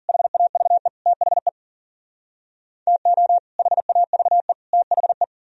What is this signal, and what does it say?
Morse code, heard as a keyed tone, about 52 wpm, 700 Hz. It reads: HAVE THE TO HAVE THE